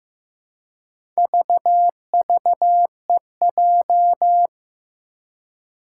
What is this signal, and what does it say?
Morse code, heard as a keyed tone, about 15 wpm, 700 Hz. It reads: VVEJ
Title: Four Challenging Letter-Number Combinations 15wpm